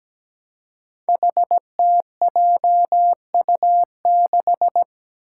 Morse code, heard as a keyed tone, about 17 wpm, 700 Hz. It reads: HTJU6
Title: Five Challenging Letter-Number Combinations 17wpm